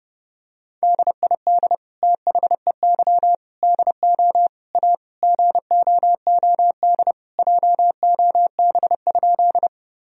Morse code, heard as a keyed tone, about 32 wpm, 700 Hz. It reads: DID THEY DO A GOOD JOB?
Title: Sentences from Top 1000 Words - Review Part 3 - 30wpm